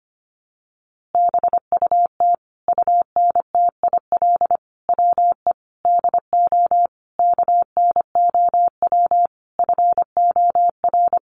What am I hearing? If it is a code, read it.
Morse code, heard as a keyed tone, about 25 wpm, 700 Hz. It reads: BUT UNTIL WE DO KNOW FOR
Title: A Sound Of Thunder 25wpm